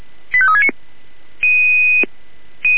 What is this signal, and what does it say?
Five-tone selective call (ZVEI), heard as quick runs of five tones, tones of about 70 ms. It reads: 85369
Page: SDR/ZVEI